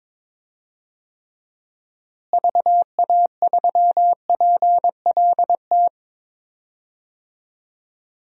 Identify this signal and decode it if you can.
Morse code, heard as a keyed tone, about 22 wpm, 700 Hz. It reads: VA3PLT
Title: DXCC-100 Call Signs - 6 Characters - Repeated 3x - 22wpm